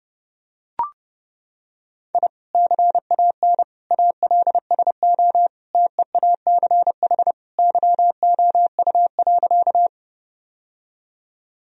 Morse code, heard as a keyed tone, about 30 wpm, 700 Hz. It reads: I CAN ALSO TEACH YOU.
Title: Sentences from Top 1000 Words - Review Part 2 - 30wpm